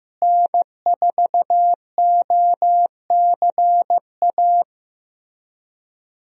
Morse code, heard as a keyed tone, about 15 wpm, 700 Hz. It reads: N4OCA
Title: USA Call Signs - 5 Characters - Repeated 3x - 15wpm